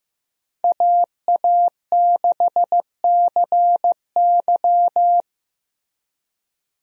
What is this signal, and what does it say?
Morse code, heard as a keyed tone, about 15 wpm, 700 Hz. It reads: AA6CY
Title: USA Call Signs - 5 Characters - Repeated 3x - 15wpm